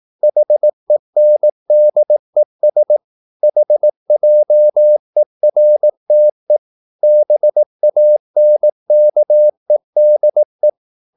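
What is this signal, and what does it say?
Morse code, heard as a keyed tone, about 18 wpm, 600 Hz. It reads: HENDES HJERTE BANKEDE
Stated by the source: Slow (QRS) Morse practice at 18 wpm, CW med Gnister.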